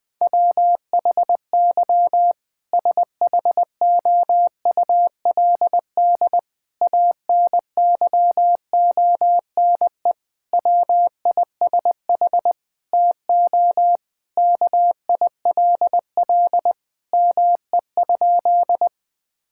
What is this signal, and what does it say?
Morse code, heard as a keyed tone, about 20 wpm, 700 Hz. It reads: WHY SHOULD ANYONE WISH TO KILL ME?
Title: Sentences from Top 1600 Words 20wpm